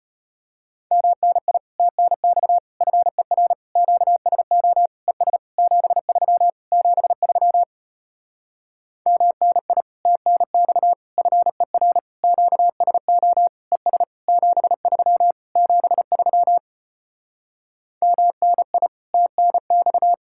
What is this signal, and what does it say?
Morse code, heard as a keyed tone, about 33 wpm, 700 Hz. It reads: MNI TNX FER QSO ES 73 73 MNI TNX FER QSO ES 73 73 MNI TNX